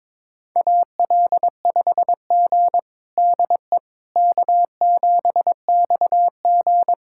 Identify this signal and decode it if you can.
Morse code, heard as a keyed tone, about 21 wpm, 700 Hz. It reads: AL5G DE K7XG
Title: USA Callsign DE Callsign - 4 Characters - 22wpm